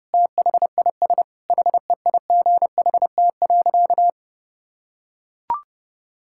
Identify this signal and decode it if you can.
Morse code, heard as a keyed tone, about 30 wpm, 700 Hz. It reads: THIS HEIGHT.
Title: Sentences from Top 1600 Words 30wpm